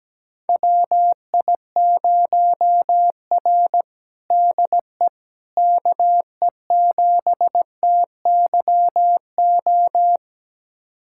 Morse code, heard as a keyed tone, about 17 wpm, 700 Hz. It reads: WI0R DE KE7TYO
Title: USA Callsign DE Callsign 17wpm